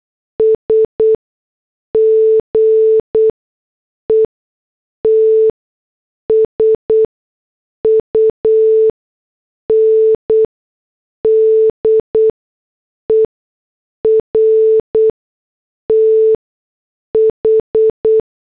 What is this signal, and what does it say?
Morse code, heard as a keyed tone, about 8 wpm, spaced as for 6 wpm, 440 Hz. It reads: S GETS UNDER TH
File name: morse / headlines